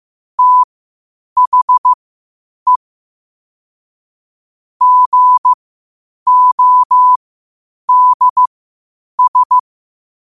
Morse code, heard as a keyed tone, about 15 wpm, spaced as for 8 wpm, 1000 Hz. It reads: THE GODS
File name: qotdfiles / 8wpm